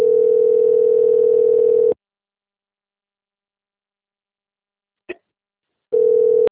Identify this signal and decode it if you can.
Ringback tone (North America), 440 + 480 Hz.
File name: out- - - - - .97898